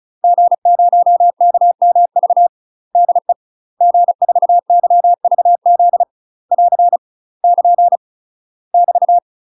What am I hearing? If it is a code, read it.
Morse code, heard as a keyed tone, about 34 wpm, 700 Hz. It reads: G0KMV DE G4YVZ <AR> <KN> <BT>
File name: UK MASTER QSOs TRAINING/35WPM